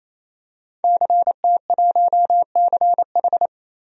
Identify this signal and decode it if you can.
Morse code, heard as a keyed tone, about 27 wpm, 700 Hz. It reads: CT1CH
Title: DXCC-100 Call Signs - 5 Characters - 28wpm